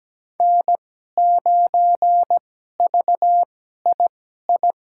Morse code, heard as a keyed tone, about 17 wpm, spaced as for 12 wpm, 700 Hz. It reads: N9VII
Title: USA Call Signs - 5 Characters - 2x Character Spacing - 17wpm